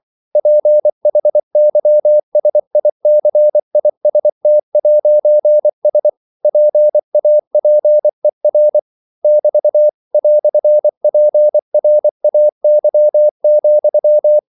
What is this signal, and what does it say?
Morse code, heard as a keyed tone, about 24 wpm, 600 Hz. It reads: PHYSICIST'S PAPER <BT> "PRAY,
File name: CW / Practice / EnglishQuotations0424